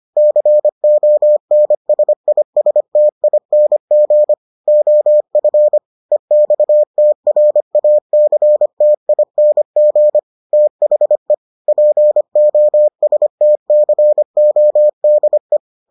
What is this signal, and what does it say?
Morse code, heard as a keyed tone, about 25 wpm, 600 Hz. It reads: CONSISTING OF EXTRACTING THE POSTCODE